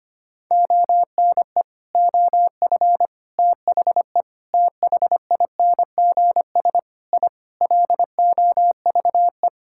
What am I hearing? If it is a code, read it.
Morse code, heard as a keyed tone, about 25 wpm, 700 Hz. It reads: ONE OF THE THINGS I LOVE
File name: practice / 25wpm / Sentences with 3-Word Phrases - 25wpm